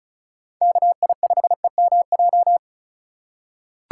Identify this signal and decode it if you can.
Morse code, heard as a keyed tone, about 36 wpm, 700 Hz. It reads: KI5EMJ
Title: USA Call Signs - 6 Characters - 3x Character Spacing - 35wpm